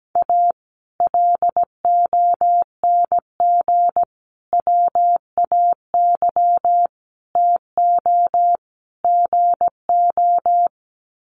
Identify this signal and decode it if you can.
Morse code, heard as a keyed tone, about 17 wpm, 700 Hz. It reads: A LONG WAY TO GO
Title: Sets of 5 Words 17wpm